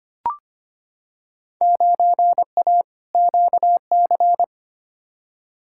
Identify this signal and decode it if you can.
Morse code, heard as a keyed tone, about 25 wpm, 700 Hz. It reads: 9A QC
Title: Field Day Exchanges Encore - 25wpm